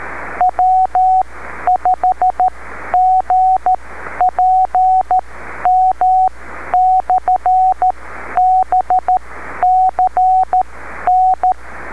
Morse code, heard as a keyed tone, about 13 wpm, spaced as for 11 wpm, 740 Hz, with radio noise. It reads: W5GPM/BCN
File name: beacons / w5gpm-b